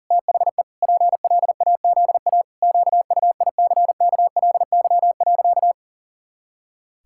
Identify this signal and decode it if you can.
Morse code, heard as a keyed tone, about 40 wpm, 700 Hz. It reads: THE PLAZA QUICKLY.
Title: Pangram Challenge v2 - 40wpm